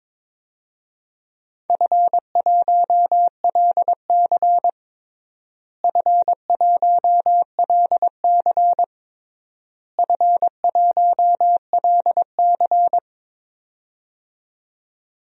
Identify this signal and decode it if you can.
Morse code, heard as a keyed tone, about 22 wpm, 700 Hz. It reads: F1LC F1LC F1LC
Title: DXCC-100 Call Signs - 4 Characters - Repeated 3x - 22wpm